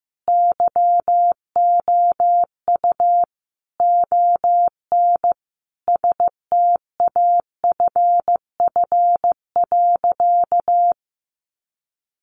Morse code, heard as a keyed tone, about 15 wpm, 700 Hz. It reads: YOU ON STAFF.
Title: Sentences from Top 1200 Words 15wpm